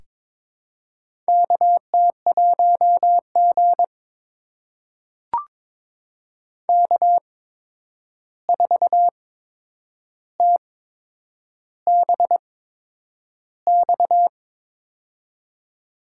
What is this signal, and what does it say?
Morse code, heard as a keyed tone, about 23 wpm, spaced as for 9 wpm, 700 Hz. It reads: KT1G K4TBX
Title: USA Call Signs - 8x Character Spacing - 22wpm